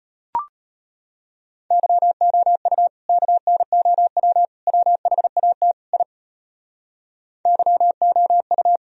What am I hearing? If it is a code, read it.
Morse code, heard as a keyed tone, about 32 wpm, 700 Hz. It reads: YOU KNOW WHAT I YOU